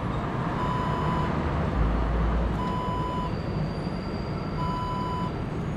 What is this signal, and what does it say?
There is no code.